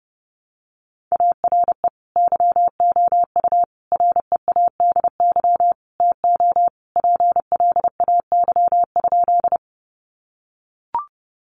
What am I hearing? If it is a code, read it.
Morse code, heard as a keyed tone, about 30 wpm, 700 Hz. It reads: ARE YOU READY TO PLAY?
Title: Sentences from Top 400 Words 30wpm